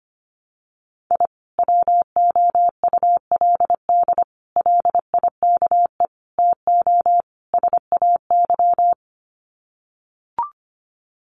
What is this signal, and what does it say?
Morse code, heard as a keyed tone, about 25 wpm, 700 Hz. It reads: I WOULD LIKE TO SAY
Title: Sets of 5 Words Encore2 25wpm